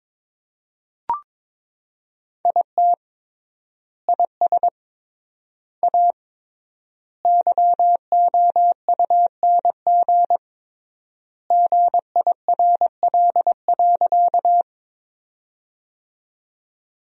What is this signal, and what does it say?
Morse code, heard as a keyed tone, about 22 wpm, 700 Hz. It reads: IT IS A YOUNG GIRL.
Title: 3x Word-Spacing - Sentences from Top 400 Words - 22wpm